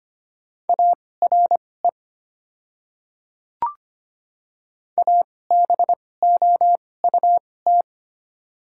Morse code, heard as a keyed tone, about 25 wpm, spaced as for 18 wpm, 700 Hz. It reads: ARE ABOUT
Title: 2x Character Spacing - Top 200 Words - 25wpm